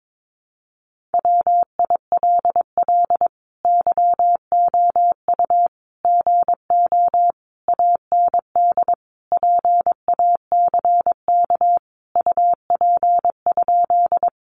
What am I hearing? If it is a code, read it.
Morse code, heard as a keyed tone, about 22 wpm, 700 Hz. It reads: WILL YOU GO AND PACK UP?
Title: Sentences from Top 1300 Words 22wpm